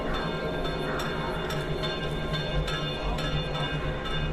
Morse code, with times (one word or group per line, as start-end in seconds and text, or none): none